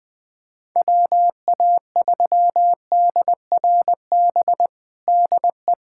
Morse code, heard as a keyed tone, about 21 wpm, 700 Hz. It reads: WA3DRB DE
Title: USA Callsign DE Callsign - 6 Characters - 20wpm